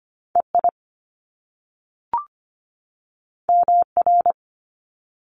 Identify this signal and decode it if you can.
Morse code, heard as a keyed tone, about 25 wpm, 700 Hz. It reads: EI MR
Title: Top Two Letter Combinations in Words 25wpm